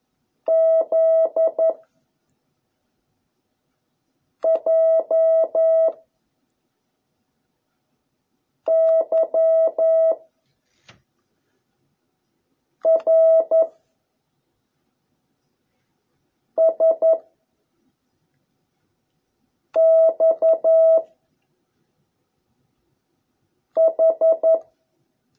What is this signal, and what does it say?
Morse code, heard as a keyed tone, about 11 wpm, 650 Hz. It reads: Z J Y R S X H